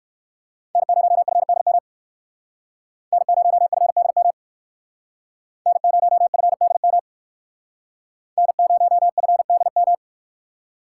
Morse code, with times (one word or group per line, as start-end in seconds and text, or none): 0.8-1.8 N0FDM
3.1-4.3 N0FDM
5.7-7.0 N0FDM
8.4-9.9 N0FDM